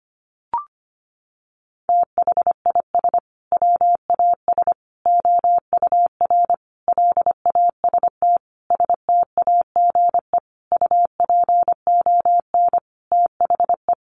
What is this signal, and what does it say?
Morse code, heard as a keyed tone, about 25 wpm, 700 Hz. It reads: THIS WAS OUR LAST STAGE UPON THE
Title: Sentences from Top 1200 Words 25wpm